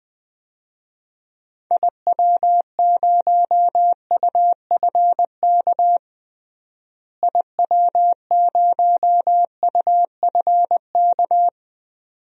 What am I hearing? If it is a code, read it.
Morse code, heard as a keyed tone, about 21 wpm, 700 Hz. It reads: IW0UFK IW0UFK
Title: DXCC-100 Call Signs - 6 Characters - Repeated 2x - 20wpm